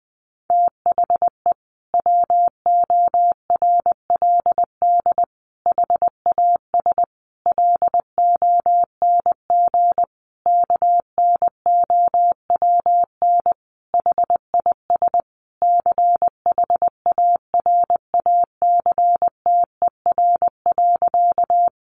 Morse code, heard as a keyed tone, about 20 wpm, 700 Hz. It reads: THE WORLD HAS LONG KNOWN HIS CHARACTER.
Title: Sentences from Top 1200 Words 20wpm